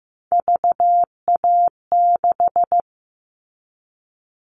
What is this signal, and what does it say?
Morse code, heard as a keyed tone, about 15 wpm, 700 Hz. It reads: VA6
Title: Callsign Prefixes for Top 50 DXCC - 15wpm